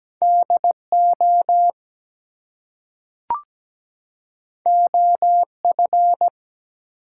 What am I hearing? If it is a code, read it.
Morse code, heard as a keyed tone, about 17 wpm, 700 Hz. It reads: DO OF
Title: Two Letter Words - 17wpm